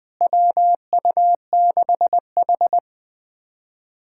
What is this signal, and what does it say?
Morse code, heard as a keyed tone, about 20 wpm, 700 Hz. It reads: WU6H
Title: USA Callsign DE Callsign - 4 Characters - 20wpm